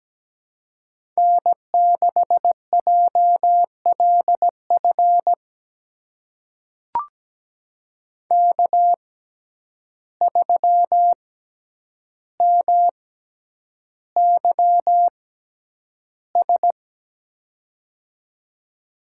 Morse code, heard as a keyed tone, about 17 wpm, spaced as for 9 wpm, 700 Hz. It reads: N6JLF K3MYS
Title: USA Call Signs - 5 Characters - 6x Character Spacing - 17wpm